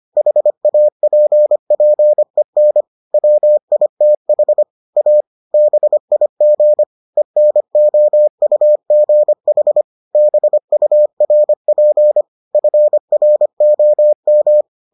Morse code, heard as a keyed tone, about 25 wpm, 600 Hz. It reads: HAPPEN WITH A BIG ENOUGH BURP FROM